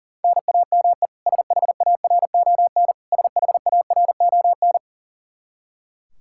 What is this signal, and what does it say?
Morse code, heard as a keyed tone, about 40 wpm, 700 Hz. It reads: NAME SHARON SHARON
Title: K1USN SST QSO Practice 40wpm